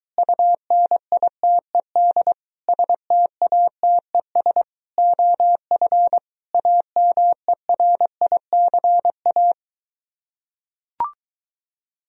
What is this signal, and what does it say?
Morse code, heard as a keyed tone, about 24 wpm, 700 Hz. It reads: UNITED STATES OF AMERICA